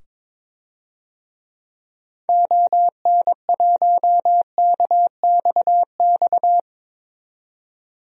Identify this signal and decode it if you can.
Morse code, heard as a keyed tone, about 21 wpm, 700 Hz. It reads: ON1KXX
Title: DXCC-100 Call Signs - 6 Characters - 22wpm